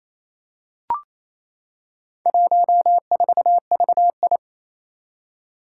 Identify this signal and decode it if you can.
Morse code, heard as a keyed tone, about 28 wpm, 700 Hz. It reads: 14VI
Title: Four Challenging Letter-Number Combinations 28wpm